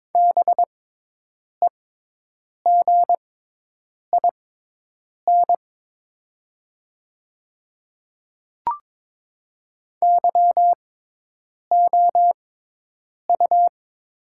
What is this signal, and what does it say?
Morse code, heard as a keyed tone, about 22 wpm, spaced as for 8 wpm, 700 Hz. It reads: BEGIN YOU
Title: 6x Character Spacing - Top 300 Words - 22wpm